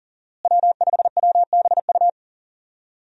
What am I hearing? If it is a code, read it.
Morse code, heard as a keyed tone, about 40 wpm, 700 Hz. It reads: W5WBU
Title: USA Call Signs - 5 Characters - Repeated 2x - 40wpm